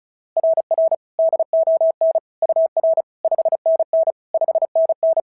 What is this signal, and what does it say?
Morse code, heard as a keyed tone, about 36 wpm, 690 Hz. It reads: RR DON UR 5NN 5NN
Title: SOTA QSO Practice 35wpm